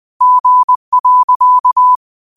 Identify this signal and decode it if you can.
Morse code, heard as a keyed tone, about 20 wpm, 1000 Hz. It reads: G.
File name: qotdfiles / 20wpm